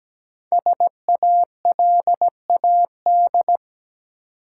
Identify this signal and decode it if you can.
Morse code, heard as a keyed tone, about 17 wpm, 700 Hz. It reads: SALAD
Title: Five Letter Words - 17wpm